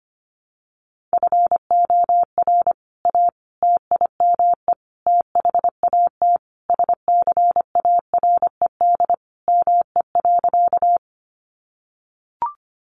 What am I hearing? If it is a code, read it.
Morse code, heard as a keyed tone, about 25 wpm, 700 Hz. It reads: FOR A TIME THAT SCARED ME.
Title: Sentences from Top 700 Words 25wpm